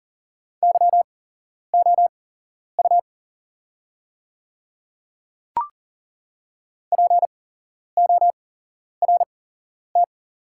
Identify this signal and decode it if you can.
Morse code, heard as a keyed tone, about 40 wpm, spaced as for 11 wpm, 690 Hz. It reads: YOU PORT
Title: 8x Character Spacing - Top 200 Words - 40wpm